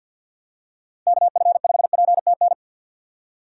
Morse code, heard as a keyed tone, about 53 wpm, 700 Hz. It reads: K F 5 P T N
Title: USA Call Signs - 6 Characters - 6x Character Spacing - 50wpm